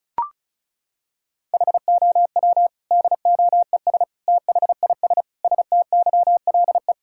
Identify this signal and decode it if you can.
Morse code, heard as a keyed tone, about 34 wpm, 700 Hz. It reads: HOW DOES THIS STYLE
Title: Sentences from Top 1000 Words - Review Part 5 - 35wpm